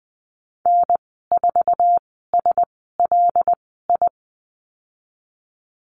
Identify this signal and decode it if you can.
Morse code, heard as a keyed tone, about 20 wpm, spaced as for 15 wpm, 700 Hz. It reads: N4SLI
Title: 082 Lesson 14g - Number 2 - Call Signs 20wpm